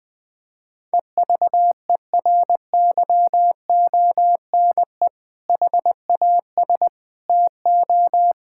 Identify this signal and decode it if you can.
Morse code, heard as a keyed tone, about 20 wpm, 700 Hz. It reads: EVERYONE HAS TO
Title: Sentences with 2-Word Phrases - Spoken Beginning - 20wpm